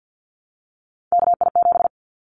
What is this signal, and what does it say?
Morse code, heard as a keyed tone, about 50 wpm, 700 Hz. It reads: KI7
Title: Callsign Prefixes for Top 25 DXCC - 50wpm